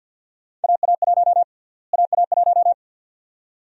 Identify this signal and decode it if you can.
Morse code, heard as a keyed tone, about 52 wpm, 700 Hz. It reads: AA1 AA1
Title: USA Call Sign Prefixes - Repeated 2x - 50wpm